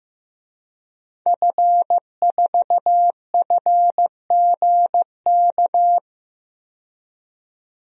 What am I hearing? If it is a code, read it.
Morse code, heard as a keyed tone, about 15 wpm, 700 Hz. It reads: F4FGK
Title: DXCC-100 Call Signs - 5 Characters - 15wpm